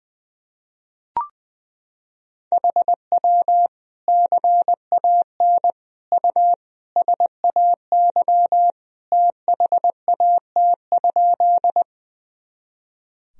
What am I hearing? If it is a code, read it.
Morse code, heard as a keyed tone, about 20 wpm, 700 Hz. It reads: HW CAN U SAY THAT?
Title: Sentences from Top 200 Words and Abbreviations 20wpm